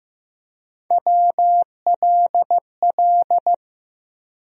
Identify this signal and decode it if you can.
Morse code, heard as a keyed tone, about 16 wpm, 700 Hz. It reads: WLL
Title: USA Call Sign Postfixes - 3 Letters 15wpm